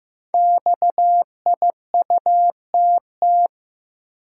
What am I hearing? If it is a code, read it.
Morse code, heard as a keyed tone, about 15 wpm, 700 Hz. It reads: XIUTT
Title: Code Groups - 5 Letters - 15wpm